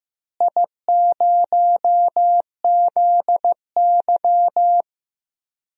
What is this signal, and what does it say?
Morse code, heard as a keyed tone, about 15 wpm, 700 Hz. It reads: I0ZY
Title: DXCC-100 Call Signs - 4 Characters - Repeated 3x - 15wpm